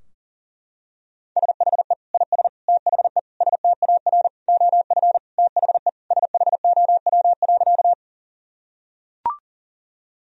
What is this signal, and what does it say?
Morse code, heard as a keyed tone, about 40 wpm, 700 Hz. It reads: SHE IS THE STAR OF THE SHOW.
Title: Sentences from Top 1000 Words - Review Part 2 - 40wpm